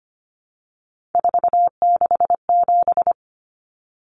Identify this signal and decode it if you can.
Morse code, heard as a keyed tone, about 25 wpm, 700 Hz. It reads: 467
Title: Numbers - 3-Digit 25wpm